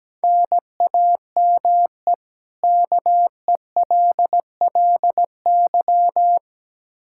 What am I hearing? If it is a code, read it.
Morse code, heard as a keyed tone, about 17 wpm, 700 Hz. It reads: NAME KELLY
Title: K1USN SST QSO Practice 17wpm